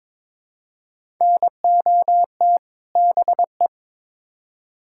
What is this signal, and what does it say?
Morse code, heard as a keyed tone, about 22 wpm, 700 Hz. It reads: NOT BE
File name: Build A Phrase - 2-Word Phrases - 22wpm